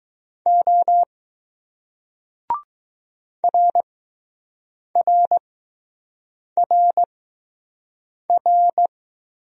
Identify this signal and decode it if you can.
Morse code, heard as a keyed tone, about 20 wpm, 700 Hz. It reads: O R R R R